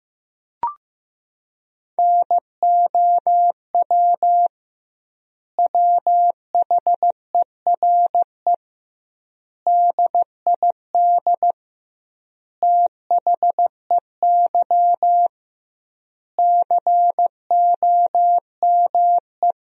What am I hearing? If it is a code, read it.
Morse code, heard as a keyed tone, about 16 wpm, 700 Hz. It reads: NOW WHERE DID THEY COME
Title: 2x Word-Spacing - Sentences from Top 200 Words - 15wpm